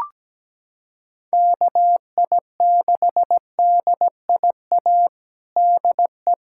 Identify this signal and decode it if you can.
Morse code, heard as a keyed tone, about 17 wpm, 700 Hz. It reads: KI6DIA DE
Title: USA Callsign DE Callsign 17wpm